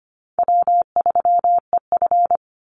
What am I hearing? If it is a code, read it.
Morse code, heard as a keyed tone, about 25 wpm, 700 Hz. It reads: W3EF
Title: Call Signs Repeated 3x - 25wpm